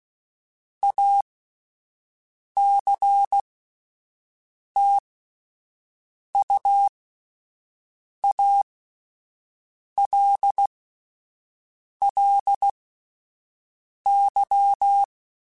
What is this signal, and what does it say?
Morse code, heard as a keyed tone, about 16 wpm, spaced as for 5 wpm, 800 Hz. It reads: ACTUALLY
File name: QSOs Spaced